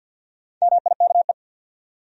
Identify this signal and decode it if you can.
Morse code, heard as a keyed tone, about 52 wpm, 700 Hz. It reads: MIKE
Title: USA Common Names - Max 5 Letters - 50wpm